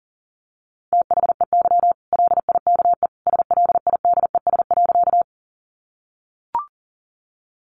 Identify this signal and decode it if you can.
Morse code, heard as a keyed tone, about 40 wpm, 700 Hz. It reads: THEY LIKE SLIDES.
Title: Easy Phrases 40wpm